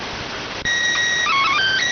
SSTV (slow-scan television), PD 290.